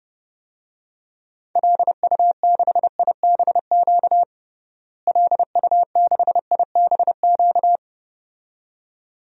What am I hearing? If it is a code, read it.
Morse code, heard as a keyed tone, about 30 wpm, 700 Hz. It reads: LU6IBQ LU6IBQ
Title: DXCC-100 Call Signs - 6 Characters - Repeated 2x - 30wpm